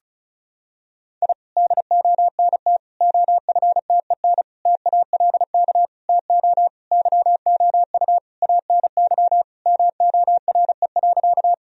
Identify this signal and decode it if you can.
Morse code, heard as a keyed tone, about 34 wpm, 700 Hz. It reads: I DONT OFTEN TALK TO YOU ANY MORE.
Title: Sentences from Top 1000 Words - Review Part 1 - 35wpm